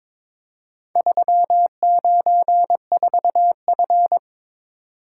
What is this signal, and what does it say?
Morse code, heard as a keyed tone, about 22 wpm, 700 Hz. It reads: 394F
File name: practice / 22wpm / Four Letter-Number Combinations - 22wpm